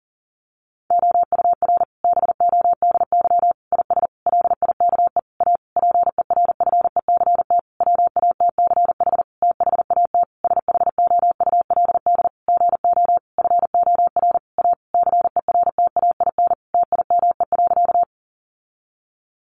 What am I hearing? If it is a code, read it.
Morse code, heard as a keyed tone, about 40 wpm, 700 Hz. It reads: OUR BODY IS LIKE A PERFECT WATCH THAT SHOULD GO FOR A CERTAIN TIME.
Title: Sentences from Top 1000 Words 40wpm